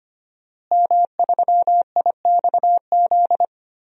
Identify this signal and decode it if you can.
Morse code, heard as a keyed tone, about 25 wpm, 700 Hz. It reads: M3IXZ
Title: DXCC-100 Call Signs - 5 Characters - Repeated 3x - 25wpm